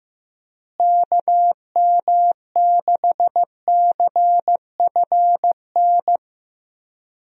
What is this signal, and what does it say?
Morse code, heard as a keyed tone, about 15 wpm, 700 Hz. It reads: KM6CFN
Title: USA Call Signs - 6 Characters - Repeated 2x - 15wpm